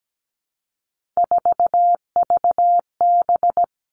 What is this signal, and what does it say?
Morse code, heard as a keyed tone, about 17 wpm, 700 Hz. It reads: 4VB
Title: Three Challenging Letter-Number Combinations 17wpm